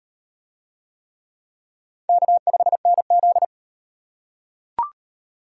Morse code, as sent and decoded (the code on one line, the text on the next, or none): -.- ..... -. --..
K5NZ